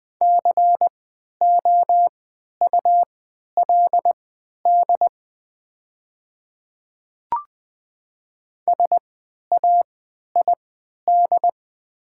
Morse code, heard as a keyed tone, about 20 wpm, spaced as for 11 wpm, 700 Hz. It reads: COULD SAID
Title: 3x Character Spacing - Top 100 Words - 20wpm